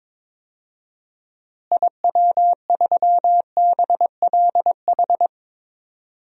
Morse code, heard as a keyed tone, about 22 wpm, 700 Hz. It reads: IW3BLH
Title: DXCC-100 Call Signs - 6 Characters - Repeated 3x - 22wpm